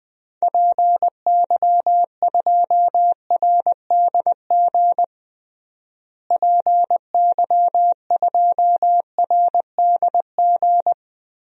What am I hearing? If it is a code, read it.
Morse code, heard as a keyed tone, about 19 wpm, 700 Hz. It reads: PY2RDG PY2RDG
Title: DXCC-100 Call Signs - 6 Characters - Repeated 3x - 20wpm